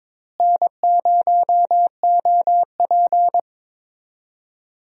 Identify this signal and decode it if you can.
Morse code, heard as a keyed tone, about 22 wpm, 700 Hz. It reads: N0OP
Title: USA Call Signs - 4 Characters - Repeated 2x - 22wpm